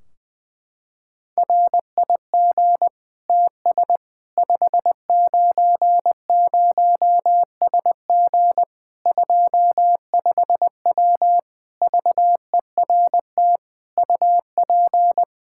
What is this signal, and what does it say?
Morse code, heard as a keyed tone, about 20 wpm, 700 Hz. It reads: RIG TS 590SG 25W VERT UP
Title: Practice QSO - Exchange 2 - Info - 20wpm